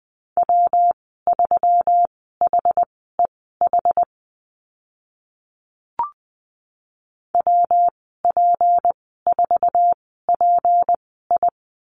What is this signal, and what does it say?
Morse code, heard as a keyed tone, about 20 wpm, spaced as for 14 wpm, 700 Hz. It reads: W3HEH WP4PI
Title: 154 Lesson 25g - Letter p - Call Signs 20wpm